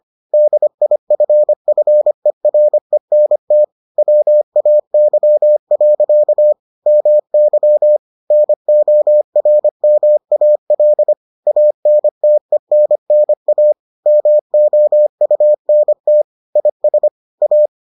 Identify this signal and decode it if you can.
Morse code, heard as a keyed tone, about 25 wpm, 600 Hz. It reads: DIFFERENT WAY. MY NORMAL ANTENNA MOUNT IS A